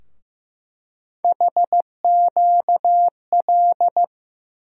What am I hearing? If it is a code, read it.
Morse code, heard as a keyed tone, about 15 wpm, 700 Hz. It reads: HQL
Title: USA Call Sign Postfixes - 3 Letters 15wpm